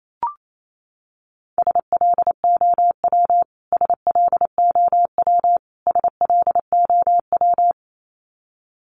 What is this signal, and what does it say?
Morse code, heard as a keyed tone, about 28 wpm, 700 Hz. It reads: SLOW SLOW SLOW
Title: Top 500 Words Repeated 28wpm